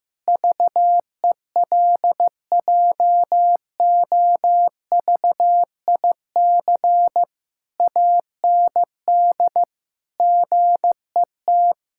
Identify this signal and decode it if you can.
Morse code, heard as a keyed tone, about 15 wpm, 700 Hz. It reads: VELJOVIC AND GET
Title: Pangram Challenge v2 - 15wpm